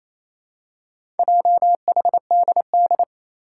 Morse code, heard as a keyed tone, about 28 wpm, 700 Hz. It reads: JHDD
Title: Four Challenging Letter-Number Combinations 28wpm